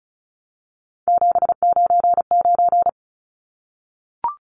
Morse code, heard as a keyed tone, about 34 wpm, 700 Hz. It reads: 799
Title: Numbers - 3-Digit 35wpm